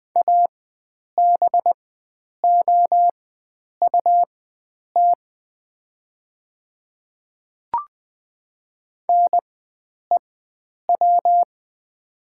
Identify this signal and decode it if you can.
Morse code, heard as a keyed tone, about 20 wpm, spaced as for 9 wpm, 700 Hz. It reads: ABOUT NEW
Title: 4x Character Spacing - Top 300 Words - 20wpm